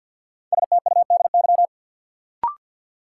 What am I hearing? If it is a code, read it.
Morse code, heard as a keyed tone, about 53 wpm, 700 Hz. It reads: STUDY
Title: Sentences from Top 300 Words - New Word Drill - 50wpm